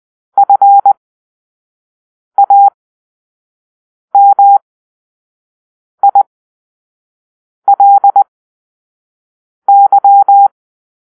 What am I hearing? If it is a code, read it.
Morse code, heard as a keyed tone, about 20 wpm, spaced as for 5 wpm, 800 Hz. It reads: FAMILY
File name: arrl / Grp-4 Drill-1